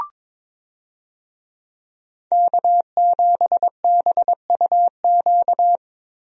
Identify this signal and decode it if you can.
Morse code, heard as a keyed tone, about 22 wpm, 700 Hz. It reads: K7BUQ
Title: USA Call Signs - 5 Characters - Repeated 2x - 22wpm